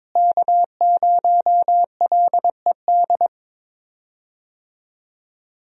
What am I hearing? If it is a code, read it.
Morse code, heard as a keyed tone, about 22 wpm, 700 Hz. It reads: K0LED
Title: USA Call Signs - 5 Characters - Repeated 2x - 22wpm